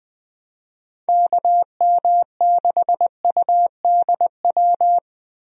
Morse code, heard as a keyed tone, about 21 wpm, 700 Hz. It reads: KM6UDW
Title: USA Call Signs - 6 Characters - Repeated 3x - 20wpm